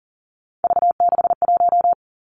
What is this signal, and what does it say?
Morse code, heard as a keyed tone, about 42 wpm, 700 Hz. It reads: V61
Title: Three Challenging Letter-Number Combinations 40wpm